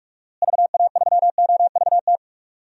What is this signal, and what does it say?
Morse code, heard as a keyed tone, about 46 wpm, 700 Hz. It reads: VA3OVT
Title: DXCC-100 Call Signs - 6 Characters - Repeated 3x - 45wpm